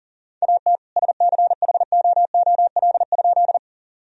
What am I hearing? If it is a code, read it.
Morse code, heard as a keyed tone, about 40 wpm, 700 Hz. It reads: AT SCHOOL?
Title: Sentences from Top 1000 Words - Review Part 1 - 40wpm